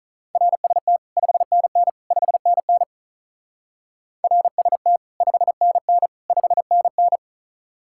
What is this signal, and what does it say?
Morse code, heard as a keyed tone, about 46 wpm, 700 Hz. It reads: RST 5NN 5NN RST 5NN 5NN